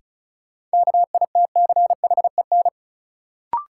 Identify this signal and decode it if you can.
Morse code, heard as a keyed tone, about 36 wpm, 700 Hz. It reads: KITCHEN
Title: Sentences from Top 1400 Words - New Word Drill - 35wpm